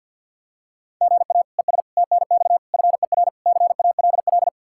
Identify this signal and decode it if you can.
Morse code, heard as a keyed tone, about 53 wpm, 700 Hz. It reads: GA ES TNX FER CALL
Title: Practice QSO - Exchange 1 - Beginning - 50wpm